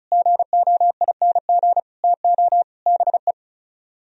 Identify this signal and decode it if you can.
Morse code, heard as a keyed tone, about 44 wpm, 700 Hz. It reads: GOING TO BE